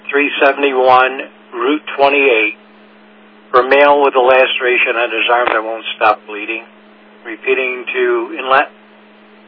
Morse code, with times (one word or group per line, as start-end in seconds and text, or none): none